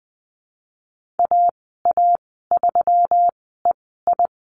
Morse code, heard as a keyed tone, about 20 wpm, spaced as for 15 wpm, 700 Hz. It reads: AA3EI
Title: 128 Lesson 21g - Number 6 - Call Signs 20wpm